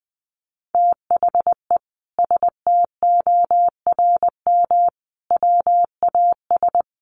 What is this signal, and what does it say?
Morse code, heard as a keyed tone, about 20 wpm, 700 Hz. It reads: THE STORM WAS
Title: Sentences from Top 1500 Words 20wpm